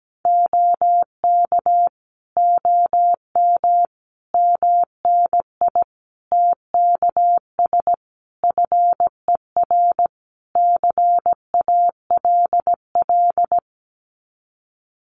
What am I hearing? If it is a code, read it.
Morse code, heard as a keyed tone, about 17 wpm, 700 Hz. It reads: OK OM MNI TKS FER CALL
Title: Common QSO Phrases 17wpm